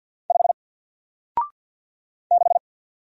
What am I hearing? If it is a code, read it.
Morse code, heard as a keyed tone, about 50 wpm, 700 Hz. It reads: 5 6
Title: Single Letter-Number 50wpm